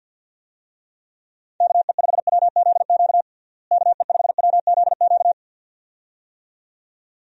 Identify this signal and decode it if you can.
Morse code, heard as a keyed tone, about 49 wpm, 700 Hz. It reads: KE5WZQ KE5WZQ
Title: USA Call Signs - 6 Characters - Repeated 2x - 50wpm